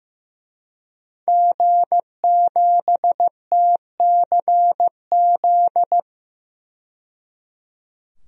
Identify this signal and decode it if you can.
Morse code, heard as a keyed tone, about 15 wpm, 700 Hz. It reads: G7TCZ
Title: DXCC-100 Call Signs - 5 Characters - 15wpm